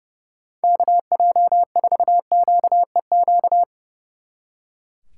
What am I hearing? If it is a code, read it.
Morse code, heard as a keyed tone, about 35 wpm, 700 Hz. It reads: KJ4QEQ
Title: Speed Racing - USA 2x3 Call Signs 30wpm